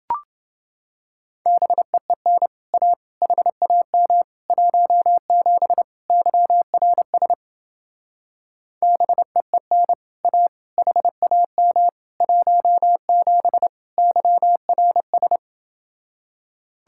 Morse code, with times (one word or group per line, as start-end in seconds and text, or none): 1.5-2.5 BEEN
2.7-2.9 A
3.2-4.2 HAM
4.5-5.8 17
6.1-7.3 YRS
8.8-9.9 BEEN
10.2-10.5 A
10.8-11.9 HAM
12.2-13.7 17
14.0-15.4 YRS